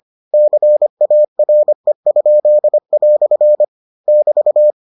Morse code, heard as a keyed tone, about 25 wpm, 600 Hz. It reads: CARE?" <BT>